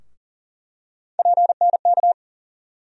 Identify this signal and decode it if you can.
Morse code, heard as a keyed tone, about 40 wpm, 700 Hz. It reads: PNK
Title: USA Call Sign Postfixes - 3 Letters 40wpm